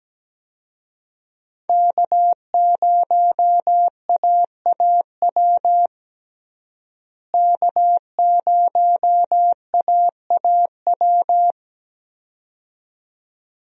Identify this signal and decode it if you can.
Morse code, heard as a keyed tone, about 17 wpm, 700 Hz. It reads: K0AAW K0AAW
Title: USA Call Signs - 5 Characters - Repeated 2x - 17wpm